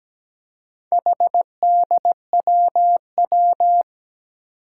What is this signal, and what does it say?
Morse code, heard as a keyed tone, about 17 wpm, 700 Hz. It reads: HDWW
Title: Four Challenging Letter-Number Combinations 17wpm